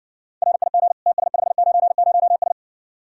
Morse code, heard as a keyed tone, about 60 wpm, 700 Hz. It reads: RIG TS590S